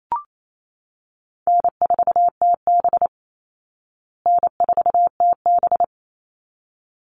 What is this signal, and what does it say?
Morse code, heard as a keyed tone, about 28 wpm, 700 Hz. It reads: N4TB N4TB
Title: Easy Call Signs Repeated 2x - 28wpm